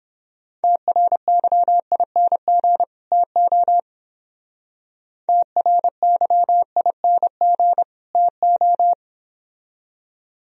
Morse code, heard as a keyed tone, about 30 wpm, 700 Hz. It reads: TRYING TO TRYING TO